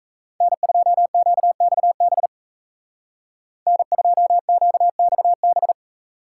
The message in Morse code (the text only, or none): N2QXB N2QXB